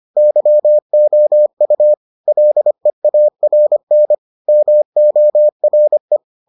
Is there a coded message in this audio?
Yes: Morse code from 0.2 to 6.2 s, heard as a keyed tone, about 25 wpm, 600 Hz. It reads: YOU LEARN MORE